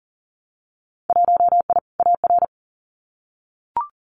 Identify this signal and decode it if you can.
Morse code, heard as a keyed tone, about 39 wpm, 700 Hz. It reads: 1I AR
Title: Winter Field Day Exchanges Encore 40wpm